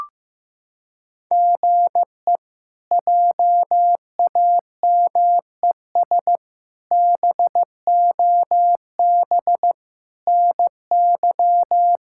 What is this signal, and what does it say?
Morse code, heard as a keyed tone, about 15 wpm, 700 Hz. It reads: GE JAMES BOB NY
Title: K1USN SST QSO Practice - 15wpm